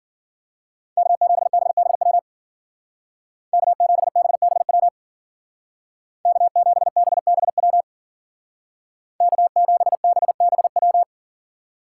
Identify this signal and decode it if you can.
Morse code, heard as a keyed tone, about 49 wpm, 700 Hz. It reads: K7BBW K7BBW K7BBW K7BBW
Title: Speed Racing - USA 1x3 Call Signs 40wpm